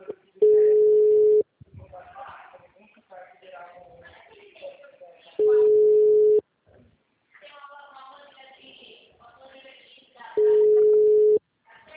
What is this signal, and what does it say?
Ringback tone (Europe, 425 Hz), with speech.
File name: out- -2019- - - .14986